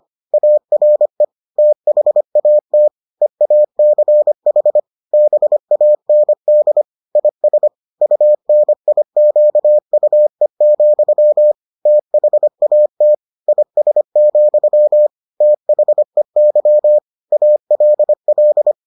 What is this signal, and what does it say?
Morse code, heard as a keyed tone, about 25 wpm, 600 Hz. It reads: ARE THAT EACH BAND IS UNIQUE, THAT IS, THEY ALL